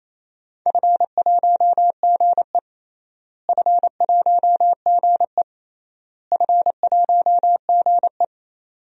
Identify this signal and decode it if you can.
Morse code, heard as a keyed tone, about 27 wpm, 700 Hz. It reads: F1GE F1GE F1GE
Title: DXCC-100 Call Signs - 4 Characters - Repeated 3x - 28wpm